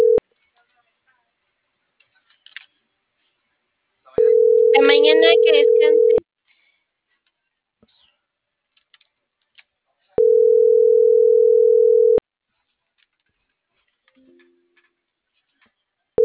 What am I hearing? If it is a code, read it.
Ringback tone (North America), 440 + 480 Hz.